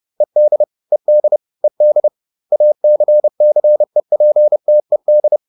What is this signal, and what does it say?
Morse code, heard as a keyed tone, about 30 wpm, 600 Hz. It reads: ED ED ED ACCEPTED